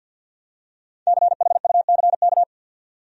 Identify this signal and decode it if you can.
Morse code, heard as a keyed tone, about 52 wpm, 700 Hz. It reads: CHUCK
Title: USA Common Names - Max 5 Letters - 50wpm